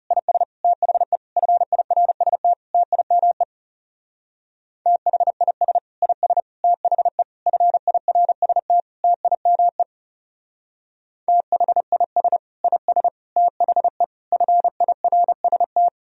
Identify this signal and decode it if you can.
Morse code, heard as a keyed tone, about 38 wpm, 700 Hz. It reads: IS THE FIRST TIME THIS IS THE FIRST TIME THIS IS THE FIRST